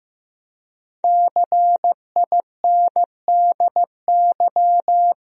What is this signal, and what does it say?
Morse code, heard as a keyed tone, about 15 wpm, 700 Hz. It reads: CINDY
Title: USA Common Names - Max 5 Letters - 15wpm